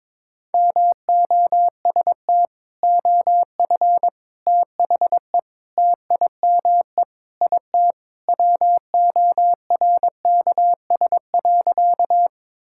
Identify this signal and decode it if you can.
Morse code, heard as a keyed tone, about 22 wpm, 700 Hz. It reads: MOST OF THE TIME IT WORKS.
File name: practice / 22wpm / Sentences with 3-Word Phrases - 22wpm